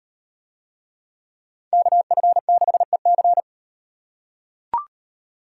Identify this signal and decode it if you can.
Morse code, heard as a keyed tone, about 35 wpm, 700 Hz. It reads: KF6EC